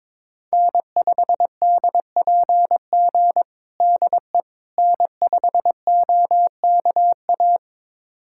Morse code, heard as a keyed tone, about 22 wpm, 700 Hz. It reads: N5DPG DE N5OKA
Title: USA Callsign DE Callsign - 5 Characters - 22wpm